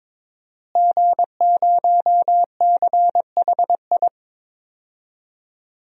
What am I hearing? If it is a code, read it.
Morse code, heard as a keyed tone, about 22 wpm, 700 Hz. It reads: G0CHI
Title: DXCC-100 Call Signs - 5 Characters - Repeated 2x - 22wpm